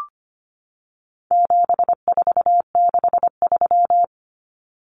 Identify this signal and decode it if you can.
Morse code, heard as a keyed tone, about 25 wpm, 700 Hz. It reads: 7463
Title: Common Cut Numbers - 4-Digit 25wpm